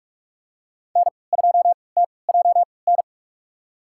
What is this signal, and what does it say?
Morse code, heard as a keyed tone, about 46 wpm, spaced as for 25 wpm, 700 Hz. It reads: N2TJN
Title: USA Call Signs - 5 Characters - 3x Character Spacing - 45wpm